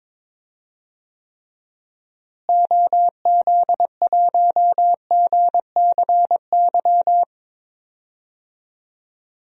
Morse code, heard as a keyed tone, about 22 wpm, 700 Hz. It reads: OZ1GCY
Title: DXCC-100 Call Signs - 6 Characters - Repeated 3x - 22wpm